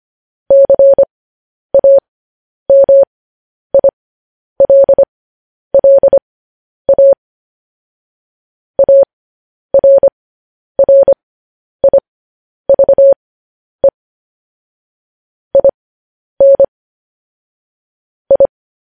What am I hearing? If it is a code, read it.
Morse code, heard as a keyed tone, about 25 wpm, spaced as for 10 wpm, 550 Hz. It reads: CAMILLA ARRIVE IN I
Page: news headlines in morse code at 10 wpm